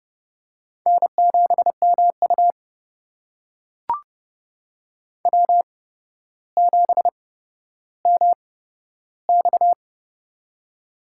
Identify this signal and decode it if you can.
Morse code, heard as a keyed tone, about 30 wpm, spaced as for 13 wpm, 700 Hz. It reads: N7MU W7MX
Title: USA Call Signs - 4 Characters - 8x Character Spacing - 30wpm